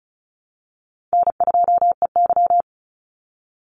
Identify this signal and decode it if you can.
Morse code, heard as a keyed tone, about 34 wpm, 700 Hz. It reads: N2EY
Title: Call Signs Repeated 3x - 35wpm